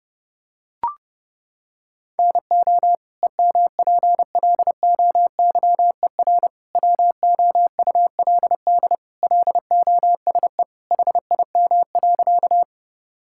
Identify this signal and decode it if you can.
Morse code, heard as a keyed tone, about 30 wpm, 700 Hz. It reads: NO EMPLOYER WOULD LOSE HIM.
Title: Sentences from Top 1600 Words 30wpm